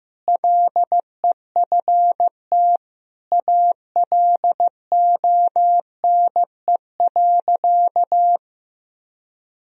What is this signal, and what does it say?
Morse code, heard as a keyed tone, about 15 wpm, 700 Hz. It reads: LEFT ALONE.
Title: Sentences from Top 1000 Words - Review Part 1 - 15wpm